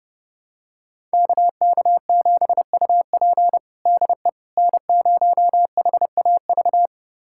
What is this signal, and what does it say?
Morse code, heard as a keyed tone, about 32 wpm, 700 Hz. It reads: KK7UP DE N0HAV
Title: USA Callsign DE Callsign - 5 Characters - 30wpm